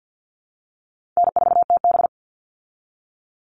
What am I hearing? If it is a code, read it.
Morse code, heard as a keyed tone, about 50 wpm, 700 Hz. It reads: N4TB
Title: Easy Call Signs Repeated 2x - 50wpm